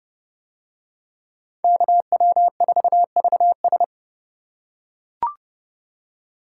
Morse code, heard as a keyed tone, about 28 wpm, 700 Hz. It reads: KW4VS